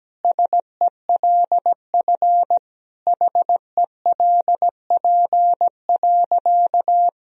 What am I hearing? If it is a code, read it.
Morse code, heard as a keyed tone, about 17 wpm, 700 Hz. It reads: SELF HELP.
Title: Sentences from Top 1000 Words - Review Part 1 - 17wpm